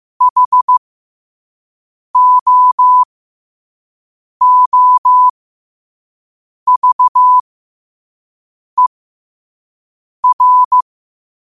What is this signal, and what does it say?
Morse code, heard as a keyed tone, about 15 wpm, spaced as for 5 wpm, 1000 Hz. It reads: HOOVER
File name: qotdfiles / 5wpm